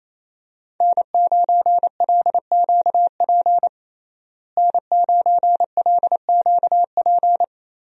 Morse code, heard as a keyed tone, about 27 wpm, 700 Hz. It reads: N9LQP N9LQP
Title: USA Call Signs - 5 Characters - Repeated 3x - 28wpm